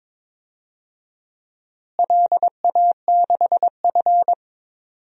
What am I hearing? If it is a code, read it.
Morse code, heard as a keyed tone, about 22 wpm, 700 Hz. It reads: LA6F
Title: DXCC-100 Call Signs - 4 Characters - Repeated 3x - 22wpm